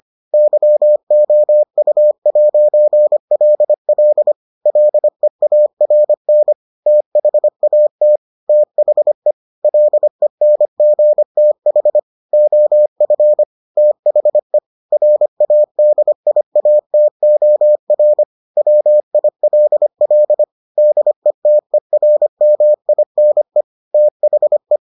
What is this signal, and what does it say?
Morse code, heard as a keyed tone, about 25 wpm, 600 Hz. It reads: YOU'LL LEARN THAT THE LENGTH OF THE RADIATOR WILL DETERMINE THE